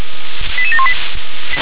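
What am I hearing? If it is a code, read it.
Five-tone selective call (ZVEI), heard as quick runs of five tones, tones of about 80 ms. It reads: 88718
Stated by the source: ZVEI five-tone selective call 88717 received by SDR.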